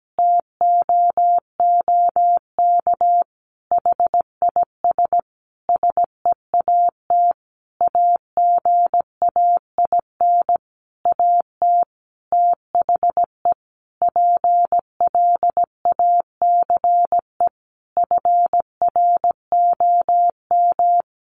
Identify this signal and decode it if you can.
Morse code, heard as a keyed tone, about 17 wpm, 700 Hz. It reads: TOOK HIS SEAT AGAIN AT THE PLACE FROM
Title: Sentences from Top 1500 Words 17wpm